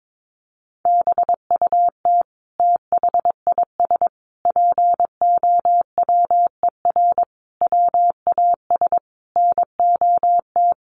Morse code, heard as a keyed tone, about 22 wpm, 700 Hz. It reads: BUT THIS POWER WAS NOT
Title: Sentences from Top 1500 Words 22wpm